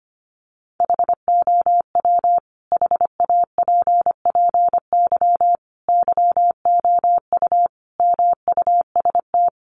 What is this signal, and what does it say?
Morse code, heard as a keyed tone, about 25 wpm, 700 Hz. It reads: HOW HAPPY YOU MUST
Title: Sentences from Top 1000 Words 25wpm